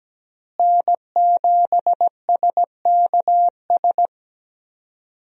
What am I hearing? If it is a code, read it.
Morse code, heard as a keyed tone, about 17 wpm, 700 Hz. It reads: N7SKS
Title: USA Call Signs - 5 Characters - Repeated 2x - 17wpm